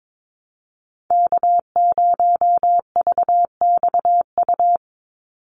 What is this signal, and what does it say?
Morse code, heard as a keyed tone, about 22 wpm, 700 Hz. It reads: K0VXU
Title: Call Signs Repeated 3x - 22wpm